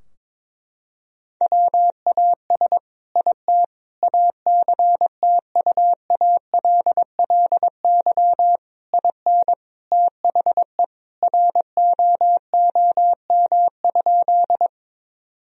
Morse code, heard as a keyed tone, about 22 wpm, 700 Hz. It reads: WAS IT ACTUALLY IN THE ROOM?
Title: Sentences from Top 1000 Words - Review Part 4 - 22wpm